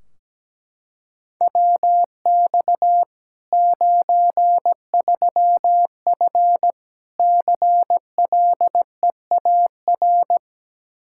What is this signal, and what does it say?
Morse code, heard as a keyed tone, about 17 wpm, 700 Hz. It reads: WX 93F CLEAR
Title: Practice QSO - Exchange 2 - Info - 17wpm